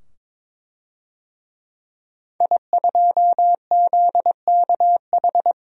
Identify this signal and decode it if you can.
Morse code, heard as a keyed tone, about 22 wpm, 700 Hz. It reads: I2ZKH
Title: DXCC-100 Call Signs - 5 Characters - 22wpm